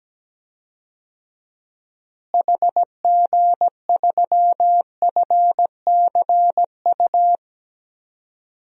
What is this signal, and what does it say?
Morse code, heard as a keyed tone, about 17 wpm, 700 Hz. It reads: HG3FCU
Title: DXCC-100 Call Signs - 6 Characters - Repeated 3x - 17wpm